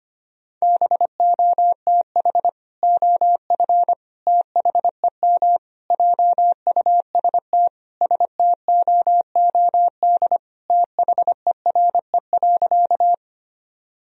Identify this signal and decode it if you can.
Morse code, heard as a keyed tone, about 25 wpm, 700 Hz. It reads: BOTH OF THEM JUST STOOD THERE.
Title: Sentences from Top 1000 Words - Review Part 2 - 25wpm